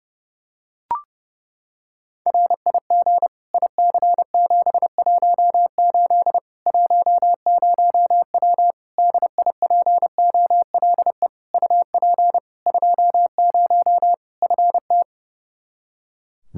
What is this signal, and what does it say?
Morse code, heard as a keyed tone, about 32 wpm, 700 Hz. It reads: RIG IC718 10W DIPOLE UP 20 FT
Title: Practice QSO - Exchange 2 - Info - 30wpm